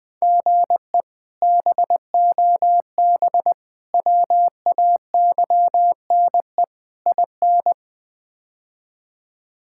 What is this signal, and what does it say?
Morse code, heard as a keyed tone, about 20 wpm, 700 Hz. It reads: GE BOB WAYNE IN
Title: K1USN SST QSO Practice - 20wpm